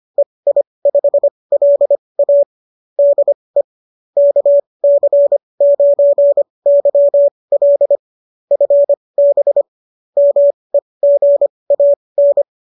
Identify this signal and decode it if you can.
Morse code, heard as a keyed tone, about 25 wpm, spaced as for 20 wpm, 570 Hz. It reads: EI5LA DE KC9YL FB MEGAN